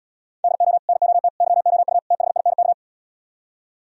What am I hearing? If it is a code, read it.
Morse code, heard as a keyed tone, about 77 wpm, 700 Hz. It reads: DO NOT YOU THINK